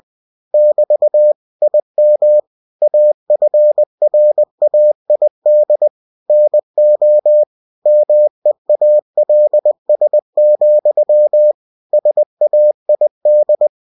Morse code, heard as a keyed tone, about 20 wpm, 600 Hz. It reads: <BT> IM AFRAID NO MEALS, SAID
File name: cw-books / winnie-the-pooh / winnie-20wpm-page-0019